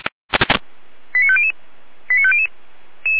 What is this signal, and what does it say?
Five-tone selective call (ZVEI), heard as quick runs of five tones, tones of about 70 ms. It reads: 89500 89500